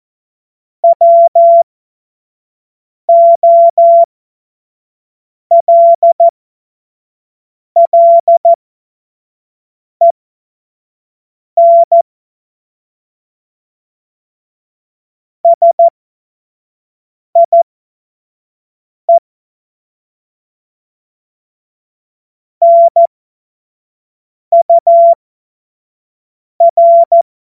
Morse code, heard as a keyed tone, about 14 wpm, spaced as for 5 wpm, 680 Hz. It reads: WOLLEN SIE NUR
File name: Morsekurs / Witze / Witz 19 März 2024 25 F70